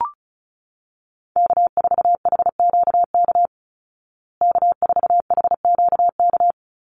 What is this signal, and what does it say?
Morse code, heard as a keyed tone, about 34 wpm, 700 Hz. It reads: K4HQK K4HQK
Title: Call Signs Repeated 2x - 35wpm